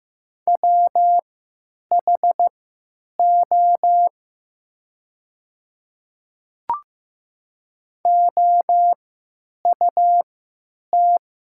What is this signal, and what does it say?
Morse code, heard as a keyed tone, about 15 wpm, spaced as for 9 wpm, 700 Hz. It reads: WHO OUT
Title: 3x Character Spacing - Top 300 Words - 15wpm